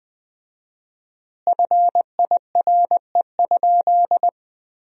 Morse code, heard as a keyed tone, about 20 wpm, 700 Hz. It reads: FIRE?
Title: 4x Word-Spacing - Sentences from Top 400 Words - 20wpm